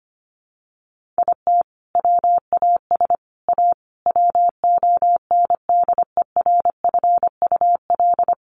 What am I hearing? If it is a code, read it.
Morse code, heard as a keyed tone, about 25 wpm, 700 Hz. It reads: IT WAS A WONDERFUL
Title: Sentences from Top 1500 Words 25wpm